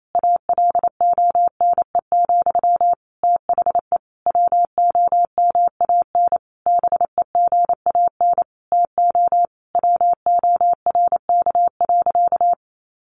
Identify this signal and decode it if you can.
Morse code, heard as a keyed tone, about 28 wpm, 700 Hz. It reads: ALONE, THE WOMAN BEGAN TO WORK.
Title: Sentences from Top 1100 Words 28wpm